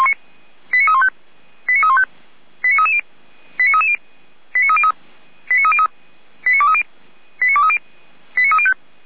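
Five-tone selective call (ZVEI), heard as quick runs of five tones, tones of about 70 ms. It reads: 89315 89315 89330 89330 89303 89303 89230 89230 89305
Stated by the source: Five-tone ZVEI selective call 89320 received on an SDR.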